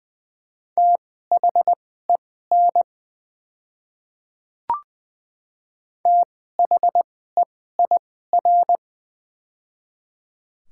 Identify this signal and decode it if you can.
Morse code, heard as a keyed tone, about 20 wpm, spaced as for 14 wpm, 700 Hz. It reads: THEN THEIR
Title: 2x Character Spacing - Top 100 Words - 20wpm